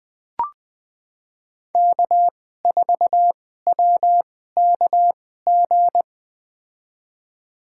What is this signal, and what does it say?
Morse code, heard as a keyed tone, about 20 wpm, spaced as for 14 wpm, 700 Hz. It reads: K4WKG
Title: USA Call Signs - 5 Characters - 2x Character Spacing - 20wpm